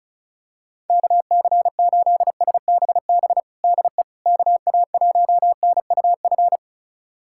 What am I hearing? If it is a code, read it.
Morse code, heard as a keyed tone, about 34 wpm, 700 Hz. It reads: KC8SBB DE KA1NUF
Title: USA Callsign DE Callsign - 6 Characters - 35wpm